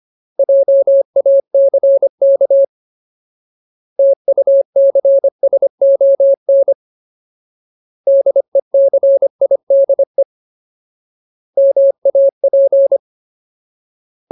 Morse code, heard as a keyed tone, about 25 wpm, 550 Hz. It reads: JACK TUCSON DECIDE MAP
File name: Ultimate Word List / 25WPM Ultimate